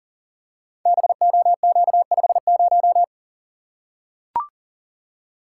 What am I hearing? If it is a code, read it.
Morse code, heard as a keyed tone, about 42 wpm, 700 Hz. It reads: BOQ50
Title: Code Groups - 5 Letter:Numbers - 40wpm